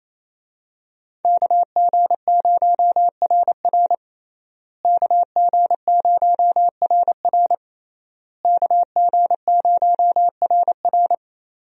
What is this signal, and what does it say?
Morse code, heard as a keyed tone, about 28 wpm, 700 Hz. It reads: KG0RR KG0RR KG0RR
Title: USA Call Signs - 5 Characters - Repeated 3x - 28wpm